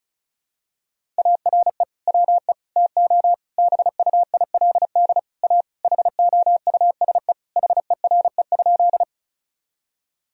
Morse code, heard as a keyed tone, about 34 wpm, 700 Hz. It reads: ARE WE TO BUILD A HOUSE HERE?
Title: Sentences from Top 1000 Words - Review Part 1 - 35wpm